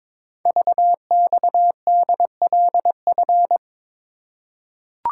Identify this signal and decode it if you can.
Morse code, heard as a keyed tone, about 23 wpm, 700 Hz. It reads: VXDLF
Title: Code Groups - 5 Letters - 22wpm